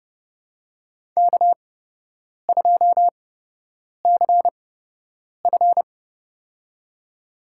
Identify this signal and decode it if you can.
Morse code, heard as a keyed tone, about 30 wpm, spaced as for 8 wpm, 700 Hz. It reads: K2CF
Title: USA Call Signs - 8x Character Spacing - 30wpm